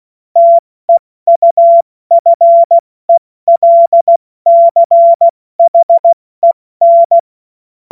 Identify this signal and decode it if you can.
Morse code, heard as a keyed tone, about 16 wpm, 680 Hz. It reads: TEUFELCHEN
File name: Morsekurs / Witze / Witz 28.November 2023 70 F80